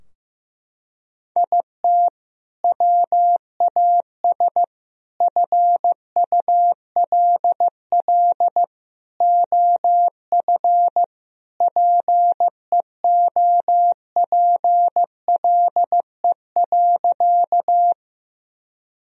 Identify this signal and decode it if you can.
Morse code, heard as a keyed tone, about 15 wpm, 700 Hz. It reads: IT WAS FULL OF PEOPLE.
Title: Sentences from Top 1000 Words - Review Part 3 - 15wpm